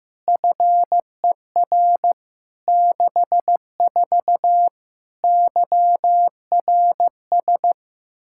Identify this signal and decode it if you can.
Morse code, heard as a keyed tone, about 16 wpm, 700 Hz. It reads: FER 64 YRS
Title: K1USN SST QSO Practice 15wpm